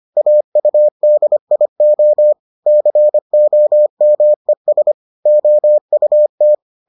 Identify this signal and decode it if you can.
Morse code, heard as a keyed tone, about 25 wpm, 600 Hz. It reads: AUDIO COMES OUT